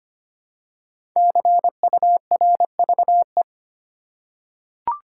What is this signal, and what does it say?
Morse code, heard as a keyed tone, about 25 wpm, 700 Hz. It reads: CURVE
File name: 5 Letter Words - First 3 Letters Given - 25wpm